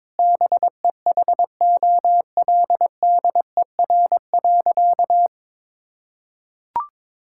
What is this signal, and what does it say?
Morse code, heard as a keyed tone, about 22 wpm, 700 Hz. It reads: BEHOLDER.
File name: Sentences with 7-Word Phrases - 22wpm